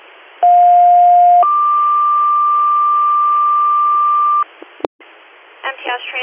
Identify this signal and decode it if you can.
Two-tone page, heard as a one-second tone, then a three-second tone, about 710 then 1180 Hz, with speech. It is heard from 0.4 to 4.4 s.